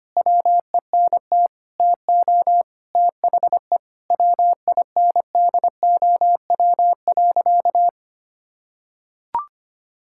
Morse code, heard as a keyed tone, about 25 wpm, 700 Hz. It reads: WENT TO THE WINDOW.
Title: Sentences from Top 1600 Words 25wpm